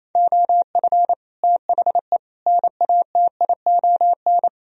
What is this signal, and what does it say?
Morse code, heard as a keyed tone, about 37 wpm, 700 Hz. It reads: OF THE NATION